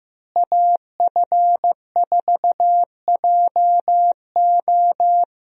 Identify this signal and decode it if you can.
Morse code, heard as a keyed tone, about 15 wpm, 700 Hz. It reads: AF4JO
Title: USA Call Signs - 5 Characters - Repeated 3x - 15wpm